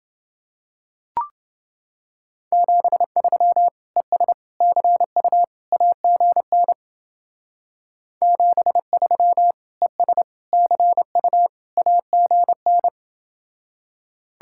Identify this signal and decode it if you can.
Morse code, heard as a keyed tone, about 28 wpm, 700 Hz. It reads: 73 ES CU AGN 73 ES CU AGN